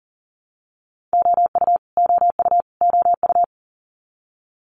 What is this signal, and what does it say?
Morse code, heard as a keyed tone, about 38 wpm, 700 Hz. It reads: OU OU OU
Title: Multi-letter Phonemes Repeated 3x 40wpm